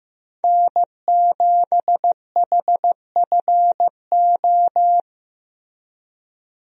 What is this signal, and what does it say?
Morse code, heard as a keyed tone, about 15 wpm, 700 Hz. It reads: N7HFO
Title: USA Call Signs - 5 Characters - Repeated 3x - 15wpm